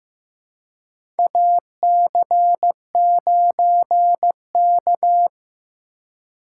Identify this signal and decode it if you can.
Morse code, heard as a keyed tone, about 15 wpm, 700 Hz. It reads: AC9K
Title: USA Call Signs - 4 Characters - 4x Character Spacing - 15wpm